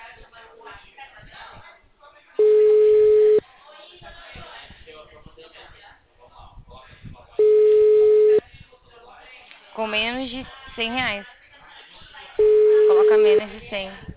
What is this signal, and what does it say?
Ringback tone (Europe, 425 Hz), with speech.
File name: out- -2015- - - .43508